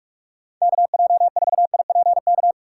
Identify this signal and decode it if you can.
Morse code, heard as a keyed tone, about 46 wpm, 700 Hz. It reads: KJ4IPK
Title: USA Call Signs - 6 Characters - 4x Character Spacing - 45wpm